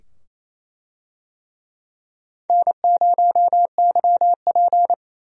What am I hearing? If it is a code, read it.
Morse code, heard as a keyed tone, about 28 wpm, 700 Hz. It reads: N0YP
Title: USA Call Signs - 4 Characters - 28wpm